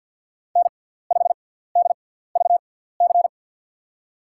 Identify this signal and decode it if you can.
Morse code, heard as a keyed tone, about 53 wpm, spaced as for 17 wpm, 700 Hz. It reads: N5DVC
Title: USA Call Signs - 5 Characters - 6x Character Spacing - 50wpm